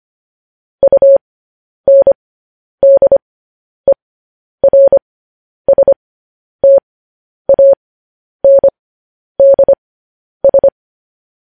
Morse code, heard as a keyed tone, about 25 wpm, spaced as for 10 wpm, 550 Hz. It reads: UNDERSTANDS
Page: news headlines in morse code at 10 wpm